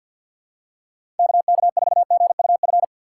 Morse code, heard as a keyed tone, about 51 wpm, 700 Hz. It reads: KK4GUF
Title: USA Call Signs - 6 Characters - 4x Character Spacing - 50wpm